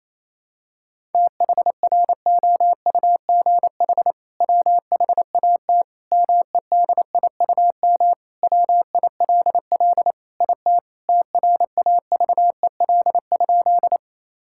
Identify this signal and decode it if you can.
Morse code, heard as a keyed tone, about 28 wpm, 700 Hz. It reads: THROUGH WHAT MEDIUM WILL IT TRAVEL?
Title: Sentences from Top 1000 Words - Review Part 6 - 28wpm